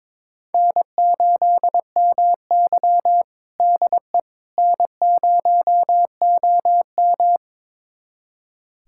Morse code, heard as a keyed tone, about 21 wpm, 700 Hz. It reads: N8MY DE N0OM
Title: USA Callsign DE Callsign - 4 Characters - 22wpm